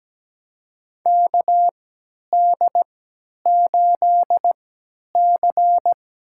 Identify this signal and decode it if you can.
Morse code, heard as a keyed tone, about 17 wpm, spaced as for 10 wpm, 700 Hz. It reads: KD8C
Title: USA Call Signs - 3x Character Spacing - 17wpm